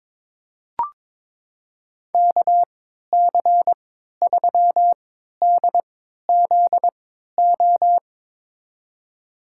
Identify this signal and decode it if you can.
Morse code, heard as a keyed tone, about 22 wpm, spaced as for 12 wpm, 700 Hz. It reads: KC3DZO
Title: USA Call Signs - 3x Character Spacing - 22wpm